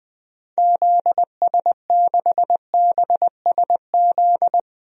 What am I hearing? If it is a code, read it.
Morse code, heard as a keyed tone, about 20 wpm, 700 Hz. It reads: ZS6BSZ
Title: DXCC-100 Call Signs - 6 Characters - 20wpm